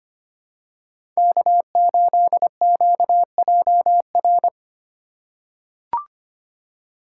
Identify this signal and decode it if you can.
Morse code, heard as a keyed tone, about 25 wpm, 700 Hz. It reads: K8QJR
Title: USA Call Signs - 5 Characters - 4x Character Spacing - 25wpm